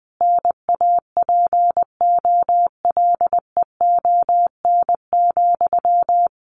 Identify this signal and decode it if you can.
Morse code, heard as a keyed tone, about 20 wpm, 700 Hz. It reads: NAPOLEON,
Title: A Sound Of Thunder 20wpm